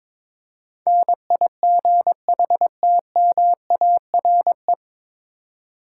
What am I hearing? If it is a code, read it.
Morse code, heard as a keyed tone, about 22 wpm, 700 Hz. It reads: NIGHTMARE
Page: Morse Code Ninja practice 22wpm